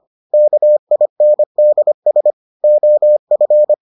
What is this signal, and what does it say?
Morse code, heard as a keyed tone, about 25 wpm, 600 Hz. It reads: KINDS OF